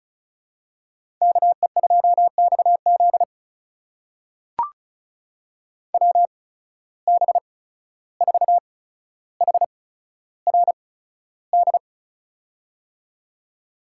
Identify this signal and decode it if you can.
Morse code, heard as a keyed tone, about 36 wpm, spaced as for 14 wpm, 700 Hz. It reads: KE2XZ WB4HRD
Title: USA Call Signs - 8x Character Spacing - 35wpm